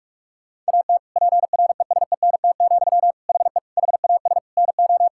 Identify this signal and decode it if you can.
Morse code, heard as a keyed tone, about 46 wpm, 700 Hz. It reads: AT PRESENT, HE HAS NO
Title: Sentences from Top 1000 Words - Review Part 5 - 45wpm